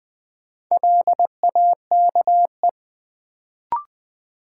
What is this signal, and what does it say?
Morse code, heard as a keyed tone, about 20 wpm, 700 Hz. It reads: LAKE
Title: Sentences from Top 1000 Words - Review Part 3 - New Word Drill - 20wpm